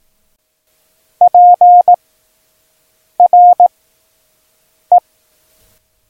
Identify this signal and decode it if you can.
Morse code, heard as a keyed tone, about 18 wpm, 710 Hz. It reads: P R E
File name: acs / Instrument Rating / Instrument ACS